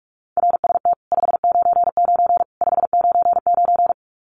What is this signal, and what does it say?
Morse code, heard as a keyed tone, about 44 wpm, 700 Hz. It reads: RST 599 599
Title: Common QSO Phrases 45wpm